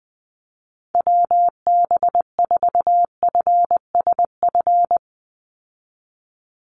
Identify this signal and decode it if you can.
Morse code, heard as a keyed tone, about 19 wpm, 700 Hz. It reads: WB4FSF
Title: Call Signs - 20wpm